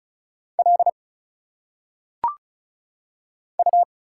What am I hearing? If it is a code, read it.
Morse code, heard as a keyed tone, about 36 wpm, 700 Hz. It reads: L U
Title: Single Letters - Phonetic Pronunciation 35wpm